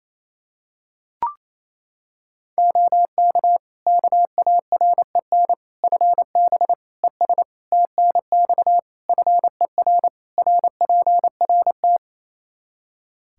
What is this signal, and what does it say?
Morse code, heard as a keyed tone, about 27 wpm, 700 Hz. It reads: OK KAREN FB ES TNX FER RPRT
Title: Practice QSO - Exchange 2 - Info - 28wpm